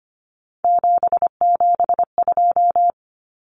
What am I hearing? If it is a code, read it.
Morse code, heard as a keyed tone, about 25 wpm, 700 Hz. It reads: 772
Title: Numbers - 3-Digit 25wpm